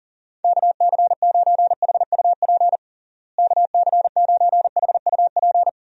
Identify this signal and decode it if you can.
Morse code, heard as a keyed tone, about 40 wpm, 700 Hz. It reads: KC9HUP KC9HUP
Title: USA Call Signs - 6 Characters - Repeated 3x - 40wpm